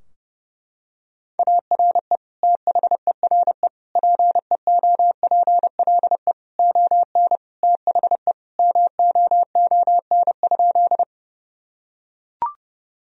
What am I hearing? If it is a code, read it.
Morse code, heard as a keyed tone, about 31 wpm, 700 Hz. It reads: ARE THERE PEOPLE ON THE MOON?
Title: Sentences from Top 1000 Words - Review Part 3 - 30wpm